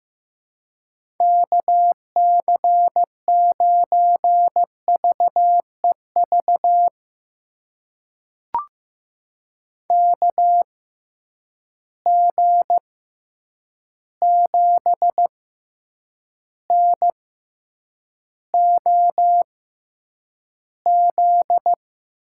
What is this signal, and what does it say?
Morse code, heard as a keyed tone, about 15 wpm, spaced as for 8 wpm, 700 Hz. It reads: KC9VEV KG7NOZ
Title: USA Call Signs - 6 Characters - 6x Character Spacing - 15wpm